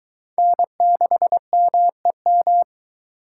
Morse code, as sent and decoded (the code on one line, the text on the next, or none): -. -.... -- . --
N6MEM